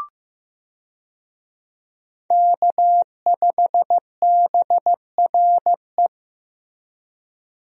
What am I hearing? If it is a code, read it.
Morse code, heard as a keyed tone, about 16 wpm, 700 Hz. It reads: K5BRE
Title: USA Call Signs - 5 Characters - Repeated 3x - 15wpm